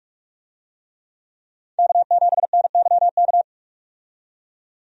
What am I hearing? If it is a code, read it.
Morse code, heard as a keyed tone, about 44 wpm, 700 Hz. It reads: K7NYK
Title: USA Call Signs - 5 Characters - 45wpm